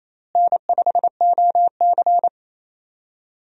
Morse code, heard as a keyed tone, about 27 wpm, 700 Hz. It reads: N5OC
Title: USA Call Signs - 4 Characters - Repeated 2x - 28wpm